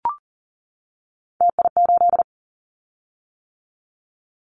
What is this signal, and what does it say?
Morse code, heard as a keyed tone, about 40 wpm, 700 Hz. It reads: TI8
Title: Callsign Prefixes for Top 100 DXCC - 40wpm